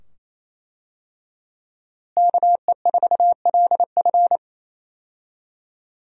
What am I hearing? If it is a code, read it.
Morse code, heard as a keyed tone, about 27 wpm, 700 Hz. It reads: KE4LF
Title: USA Call Signs - 5 Characters - Repeated 2x - 28wpm